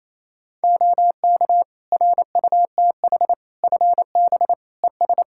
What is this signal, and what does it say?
Morse code, heard as a keyed tone, about 28 wpm, 700 Hz. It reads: OK RUTH FB ES
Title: Practice QSO - Exchange 2 - Info - 28wpm